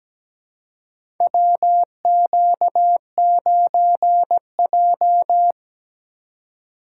Morse code, heard as a keyed tone, about 17 wpm, 700 Hz. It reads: WQ9J
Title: USA Call Signs - 4 Characters - 3x Character Spacing - 17wpm